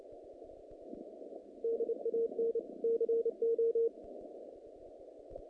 Morse code, heard as a keyed tone, about 28 wpm, 460 Hz. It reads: DAN CO
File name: contests / naqp-cw-k5tr-op-k5pi- -2100